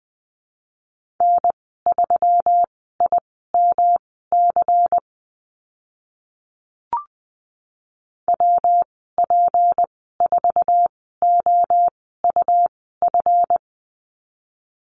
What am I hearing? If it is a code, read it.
Morse code, heard as a keyed tone, about 20 wpm, spaced as for 14 wpm, 700 Hz. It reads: N3IMC WP4OUF
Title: 161 Lesson 26g - Letter g - Call Signs 20wpm